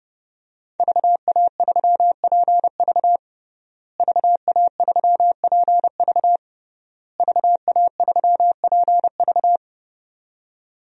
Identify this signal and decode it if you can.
Morse code, heard as a keyed tone, about 30 wpm, 700 Hz. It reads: VA3PV VA3PV VA3PV
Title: DXCC-100 Call Signs - 5 Characters - Repeated 3x - 30wpm